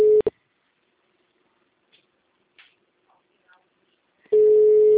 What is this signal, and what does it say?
Ringback tone (Europe, 425 Hz).